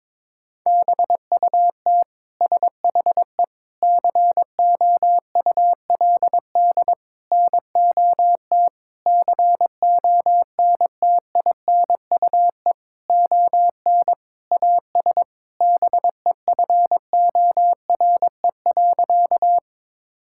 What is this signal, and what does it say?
Morse code, heard as a keyed tone, about 22 wpm, 700 Hz. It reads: BUT SHE COULD NOT CONTINUE ON AS BEFORE.
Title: Sentences from Top 1000 Words - Review Part 6 - 22wpm